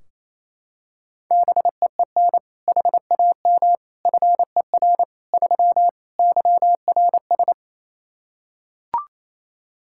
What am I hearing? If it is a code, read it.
Morse code, heard as a keyed tone, about 28 wpm, 700 Hz. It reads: BEEN HAM FER 3 YRS
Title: Practice QSO - Exchange 3 - More Info - 28wpm